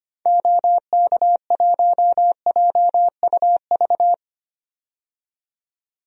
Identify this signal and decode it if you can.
Morse code, heard as a keyed tone, about 25 wpm, 700 Hz. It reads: OK1JUV
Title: DXCC-100 Call Signs - 6 Characters - 25wpm